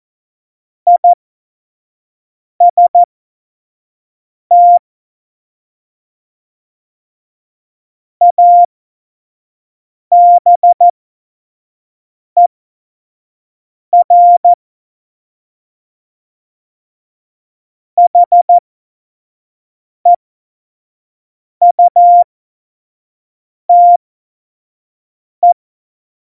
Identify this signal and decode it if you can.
Morse code, heard as a keyed tone, about 14 wpm, spaced as for 5 wpm, 700 Hz. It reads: IST ABER HEUTE
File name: Morsekurs / Witze / Witz 16. Oktober 2018, 25 F70